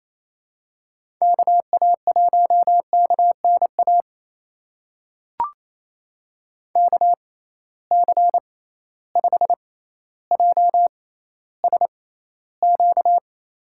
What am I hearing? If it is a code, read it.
Morse code, heard as a keyed tone, about 27 wpm, spaced as for 14 wpm, 700 Hz. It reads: KA1KNA KC5JSQ
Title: USA Call Signs - 6 Characters - 6x Character Spacing - 28wpm